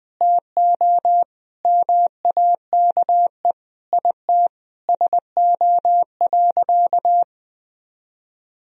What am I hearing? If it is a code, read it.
Morse code, heard as a keyed tone, about 19 wpm, 700 Hz. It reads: TO MAKE IT SO.
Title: Sentences from Top 1000 Words - Review Part 3 - 20wpm